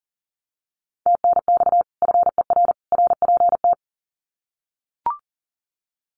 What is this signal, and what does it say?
Morse code, heard as a keyed tone, about 40 wpm, 700 Hz. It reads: TNX FER RPT
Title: Common QSO Phrases 40wpm